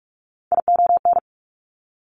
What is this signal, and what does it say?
Morse code, heard as a keyed tone, about 46 wpm, 700 Hz. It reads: ION
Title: Top Three Letter Combinations in Words 45wpm